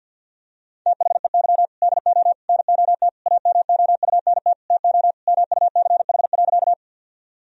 Morse code, heard as a keyed tone, about 50 wpm, 700 Hz. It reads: THEY DO NOT AMOUNT TO MUCH.
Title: Sentences from Top 1000 Words - Review Part 4 - 50wpm